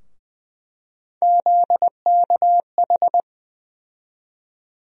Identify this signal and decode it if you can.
Morse code, heard as a keyed tone, about 21 wpm, 700 Hz. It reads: ZKH
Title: USA Call Sign Postfixes - 3 Letters 20wpm